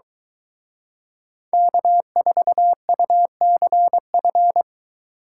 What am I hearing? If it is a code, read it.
Morse code, heard as a keyed tone, about 23 wpm, 700 Hz. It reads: K4UCF